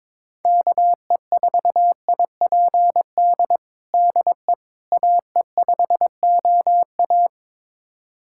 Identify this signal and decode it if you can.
Morse code, heard as a keyed tone, about 22 wpm, 700 Hz. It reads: KE4IPD DE AE5OA
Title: USA Callsign DE Callsign 22wpm